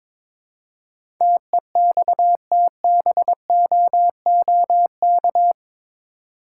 Morse code, heard as a keyed tone, about 22 wpm, 700 Hz. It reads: TEXTBOOK
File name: practice / 22wpm / Eight Letter Words - 22wpm